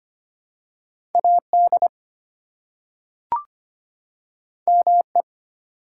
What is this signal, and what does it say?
Morse code, heard as a keyed tone, about 25 wpm, 700 Hz. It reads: AD ME
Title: Two Letter Words - 25wpm